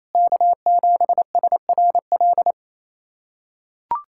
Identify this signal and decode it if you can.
Morse code, heard as a keyed tone, about 28 wpm, 700 Hz. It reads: K7SRL
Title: SOTA QSO Practice 28wpm